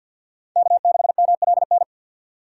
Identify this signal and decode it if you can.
Morse code, heard as a keyed tone, about 53 wpm, 700 Hz. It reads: K6MLN
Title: USA Call Signs - 4x Character Spacing - 50wpm